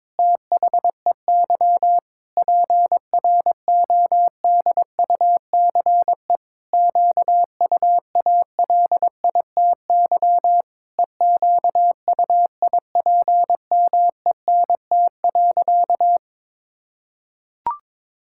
Morse code, heard as a keyed tone, about 22 wpm, 700 Hz. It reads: THEY PRODUCE QUALITY EQUIPMENT.
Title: Sentences from Top 1000 Words - Review Part 6 - 22wpm